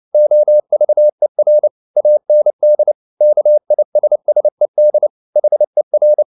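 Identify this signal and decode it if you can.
Morse code, heard as a keyed tone, about 29 wpm, 600 Hz. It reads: OVER AND KISSED HER
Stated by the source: Morse code at 29 wpm, sending Ken Follett's novel Never in English, from CW med Gnister.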